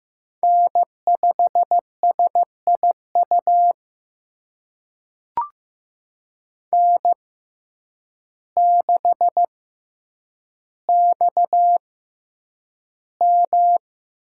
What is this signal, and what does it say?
Morse code, heard as a keyed tone, about 15 wpm, spaced as for 8 wpm, 700 Hz. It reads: N5SIU N6XM
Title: USA Call Signs - 5 Characters - 6x Character Spacing - 15wpm